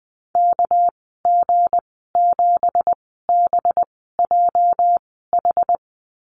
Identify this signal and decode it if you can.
Morse code, heard as a keyed tone, about 20 wpm, spaced as for 15 wpm, 700 Hz. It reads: KG7BJH
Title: 231 Lesson 37g - Letter x - Call Signs 20wpm